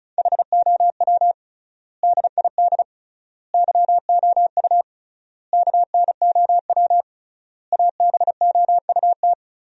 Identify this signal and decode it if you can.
Morse code, heard as a keyed tone, about 34 wpm, 700 Hz. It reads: HOW DID YOU KNOW ABOUT
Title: 3x Word-Spacing - Sentences from Top 100 Words - 35wpm